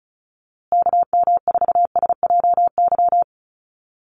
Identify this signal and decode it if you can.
Morse code, heard as a keyed tone, about 34 wpm, 700 Hz. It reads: KM4SJY
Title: USA Call Signs Repeated 2x 35wpm